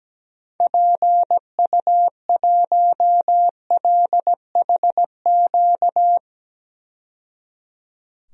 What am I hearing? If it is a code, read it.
Morse code, heard as a keyed tone, about 17 wpm, 700 Hz. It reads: PU1LHQ
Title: DXCC-100 Call Signs - 6 Characters - 17wpm